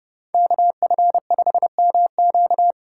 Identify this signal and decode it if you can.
Morse code, heard as a keyed tone, about 29 wpm, 700 Hz. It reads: KF5MQ